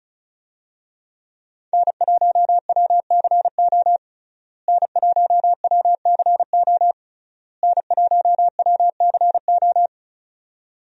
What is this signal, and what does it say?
Morse code, heard as a keyed tone, about 34 wpm, 700 Hz. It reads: N1WCO N1WCO N1WCO
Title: USA Call Signs - 5 Characters - Repeated 3x - 35wpm